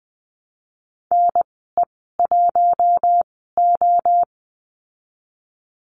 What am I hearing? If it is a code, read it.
Morse code, heard as a keyed tone, about 20 wpm, spaced as for 15 wpm, 700 Hz. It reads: NE1O
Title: 048 Lesson 9g - Number 4 - Call Signs 20wpm